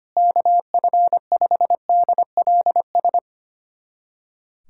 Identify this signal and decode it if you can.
Morse code, heard as a keyed tone, about 25 wpm, 700 Hz. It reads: KF5DLS
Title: USA Callsign DE Callsign - 6 Characters - 25wpm